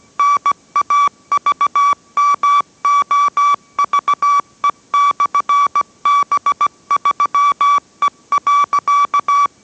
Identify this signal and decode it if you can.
Morse code, heard as a keyed tone, about 17 wpm, 1150 Hz. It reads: NAVMOVE/B3E.